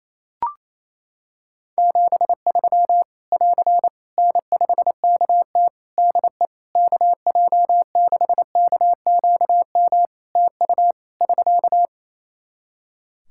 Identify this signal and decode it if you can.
Morse code, heard as a keyed tone, about 28 wpm, 700 Hz. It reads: 73 <AR> N5KT DE KJ6KQM TU <SK>
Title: Practice QSO - Exchange 4 - Ending - 28wpm